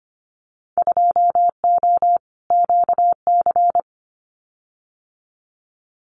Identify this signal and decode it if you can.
Morse code, heard as a keyed tone, about 25 wpm, 700 Hz. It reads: 2O QC
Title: Winter Field Day Exchanges 25wpm